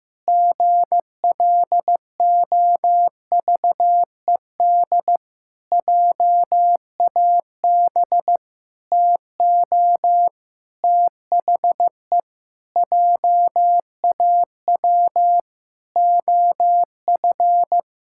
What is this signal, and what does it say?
Morse code, heard as a keyed tone, about 15 wpm, 700 Hz. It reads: GLOVED JAB TO THE JAW OF
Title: Pangram Challenge v2 - 15wpm